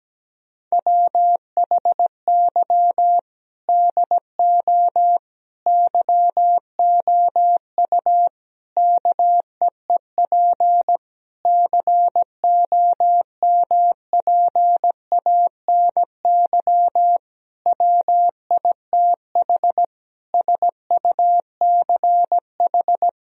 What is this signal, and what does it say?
Morse code, heard as a keyed tone, about 17 wpm, 700 Hz. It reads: WHY DO YOU KEEP COMPANY WITH SUCH
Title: Sentences from Top 1000 Words - Review Part 3 - 17wpm